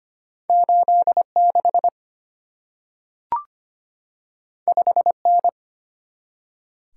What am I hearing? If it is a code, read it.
Morse code, heard as a keyed tone, about 25 wpm, 690 Hz. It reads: 86 5N
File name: Two Letter-Number Combinations - 25wpm